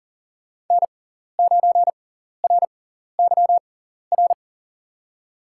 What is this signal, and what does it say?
Morse code, heard as a keyed tone, about 40 wpm, spaced as for 14 wpm, 700 Hz. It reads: N9RYR
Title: USA Call Signs - 6x Character Spacing - 40wpm